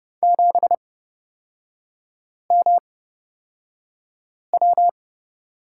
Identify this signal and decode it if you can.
Morse code, heard as a keyed tone, about 30 wpm, spaced as for 5 wpm, 700 Hz. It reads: 7MW
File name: Lesson 37e - Letter x - Three Characters 30wpm